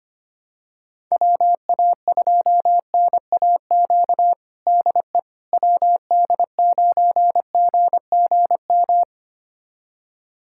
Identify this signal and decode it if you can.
Morse code, heard as a keyed tone, about 25 wpm, 700 Hz. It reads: WA2NAQ DE WD9GGM
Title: USA Callsign DE Callsign - 6 Characters - 25wpm